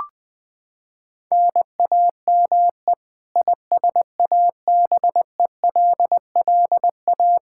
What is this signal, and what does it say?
Morse code, heard as a keyed tone, about 20 wpm, 700 Hz. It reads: NAME ISABELLA
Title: K1USN SST QSO Practice 20wpm